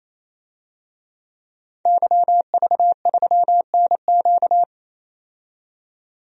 Morse code, heard as a keyed tone, about 28 wpm, 700 Hz. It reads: YV3NQ
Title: DXCC-100 Call Signs - 5 Characters - 28wpm